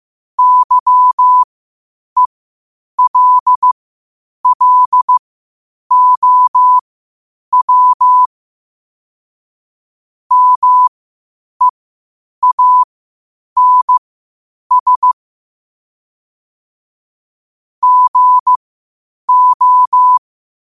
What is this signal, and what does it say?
Morse code, heard as a keyed tone, about 15 wpm, spaced as for 8 wpm, 1000 Hz. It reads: YELLOW MEANS GO
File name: qotdfiles / 8wpm